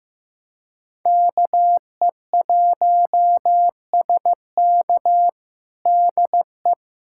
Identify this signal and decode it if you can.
Morse code, heard as a keyed tone, about 15 wpm, 700 Hz. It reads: KE1SK DE
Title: USA Callsign DE Callsign 15wpm